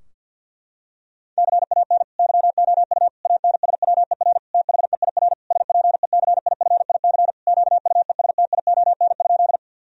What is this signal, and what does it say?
Morse code, heard as a keyed tone, about 50 wpm, 700 Hz. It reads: CAN YOU ANSWER THEIR SPECIFIC QUESTION?
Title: Sentences from Top 1000 Words - Review Part 3 - 50wpm